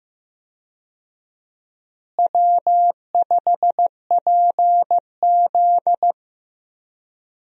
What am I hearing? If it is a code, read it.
Morse code, heard as a keyed tone, about 15 wpm, 700 Hz. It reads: W5PZ
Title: USA Call Signs - 4 Characters - Repeated 3x - 15wpm